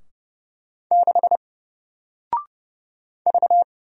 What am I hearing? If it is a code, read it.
Morse code, heard as a keyed tone, about 30 wpm, 700 Hz. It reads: 6 V
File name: Single-Letter-Number-30wpm